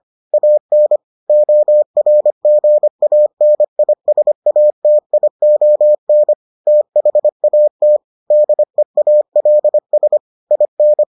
Morse code, heard as a keyed tone, about 25 wpm, 600 Hz. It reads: AN ORGANISATION THAT DEALS IN